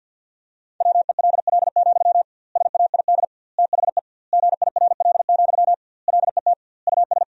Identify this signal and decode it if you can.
Morse code, heard as a keyed tone, about 50 wpm, 700 Hz. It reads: WELL, SAID THE GIRL, LET US
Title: 2x Word-Spacing - Sentences from Top 400 Words - 50wpm